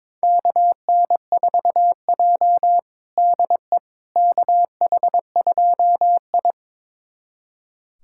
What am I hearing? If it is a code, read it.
Morse code, heard as a keyed tone, about 22 wpm, 700 Hz. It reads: KN4J DE KH2I
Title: USA Callsign DE Callsign - 4 Characters - 22wpm